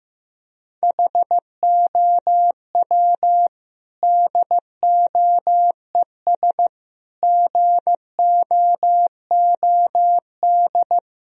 Morse code, heard as a keyed tone, about 16 wpm, 700 Hz. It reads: HOW DOES GOOD
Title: Sentences from Top 1000 Words - Review Part 1 - 15wpm